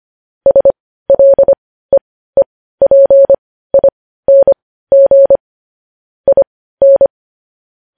Morse code, heard as a keyed tone, about 25 wpm, spaced as for 15 wpm, 550 Hz. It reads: SLEEPING IN
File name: podcast / News headlines-1x-25x15